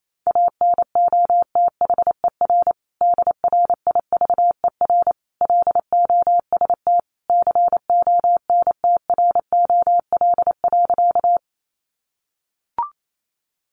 Morse code, heard as a keyed tone, about 28 wpm, 700 Hz. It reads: ANOTHER DRIVER LOST CONTROL.
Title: Sentences from Top 1400 Words 28wpm